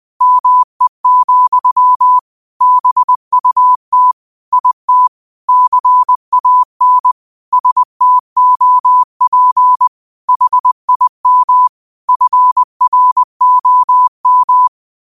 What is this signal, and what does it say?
Morse code, heard as a keyed tone, about 20 wpm, 1000 Hz. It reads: ME, BUT IT CAN STOP HIM FROM
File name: 20wpm